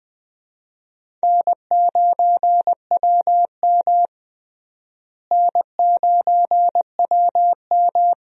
Morse code, heard as a keyed tone, about 20 wpm, 700 Hz. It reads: N9WM N9WM
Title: USA Call Signs - 4 Characters - Repeated 2x - 20wpm